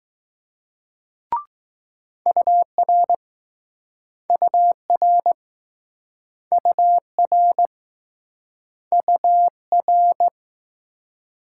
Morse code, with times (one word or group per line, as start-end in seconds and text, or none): 2.3-3.2 UR
4.3-5.3 UR
6.5-7.7 UR
8.9-10.3 UR